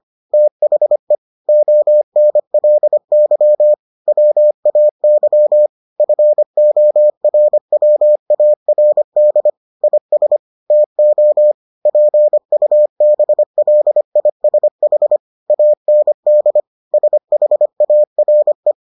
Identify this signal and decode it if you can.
Morse code, heard as a keyed tone, about 25 wpm, 600 Hz. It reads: THE ONLY WAY FORWARD IS TO PUBLISH AND SHARE